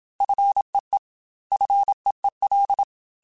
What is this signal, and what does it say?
Morse code, heard as a keyed tone, about 27 wpm, 800 Hz. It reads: FEE FEEL
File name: Single F mp3s / Single F Prefixes 3x2 Twice @25